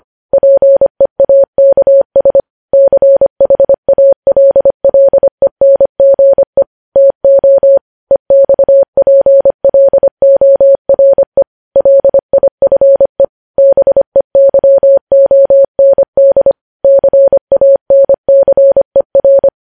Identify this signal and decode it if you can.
Morse code, heard as a keyed tone, about 25 wpm, 550 Hz. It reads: PEAKS CHALLENGE TO EXPLORE LIFE BEYOND CANCER
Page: news-headlines-in-morse-code-at-25-wpm